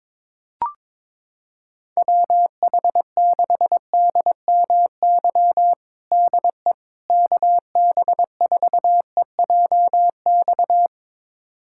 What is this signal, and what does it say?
Morse code, heard as a keyed tone, about 21 wpm, 700 Hz. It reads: WH6DMY DE KB4EJX
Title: USA Callsign DE Callsign - 6 Characters - 22wpm